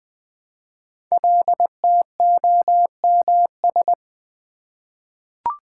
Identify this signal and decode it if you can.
Morse code, heard as a keyed tone, about 20 wpm, 700 Hz. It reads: LTOMS
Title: Code Groups - 5 Letter:Numbers - 20wpm